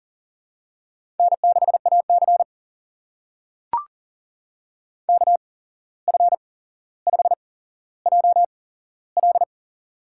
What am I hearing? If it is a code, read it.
Morse code, heard as a keyed tone, about 40 wpm, spaced as for 16 wpm, 700 Hz. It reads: N6AC KF5JL
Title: USA Call Signs - 8x Character Spacing - 40wpm